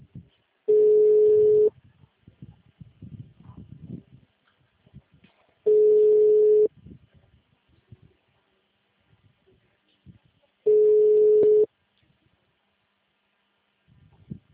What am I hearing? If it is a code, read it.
Ringback tone (Europe, 425 Hz).